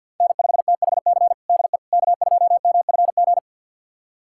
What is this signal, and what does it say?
Morse code, heard as a keyed tone, about 51 wpm, 700 Hz. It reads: N5THC DE K2MFG
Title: USA Callsign DE Callsign - 5 Characters - 50wpm